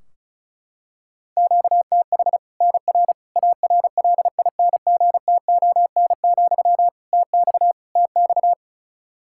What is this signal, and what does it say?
Morse code, heard as a keyed tone, about 34 wpm, 700 Hz. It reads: QTH NR ARLINGTON, TX TX
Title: Practice QSO - Exchange 1 - Beginning - 35wpm